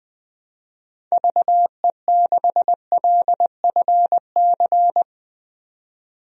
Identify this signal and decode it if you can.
Morse code, heard as a keyed tone, about 20 wpm, 700 Hz. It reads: VE6LFC
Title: DXCC-100 Call Signs - 6 Characters - 20wpm